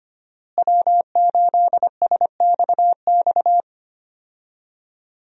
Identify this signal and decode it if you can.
Morse code, heard as a keyed tone, about 25 wpm, 700 Hz. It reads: W8SXX
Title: USA Call Signs - 5 Characters - Repeated 3x - 25wpm